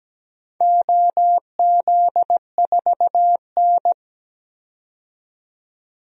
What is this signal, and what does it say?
Morse code, heard as a keyed tone, about 17 wpm, 700 Hz. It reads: OZ4N
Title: DXCC-100 Call Signs - 4 Characters - 17wpm